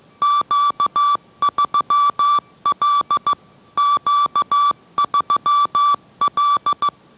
Morse code, heard as a keyed tone, about 15 wpm, 1160 Hz. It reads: Q3L Q3L